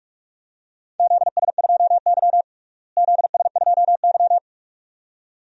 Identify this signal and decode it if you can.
Morse code, heard as a keyed tone, about 42 wpm, 700 Hz. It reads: ZS2Y ZS2Y
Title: DXCC-100 Call Signs - 4 Characters - Repeated 2x - 45wpm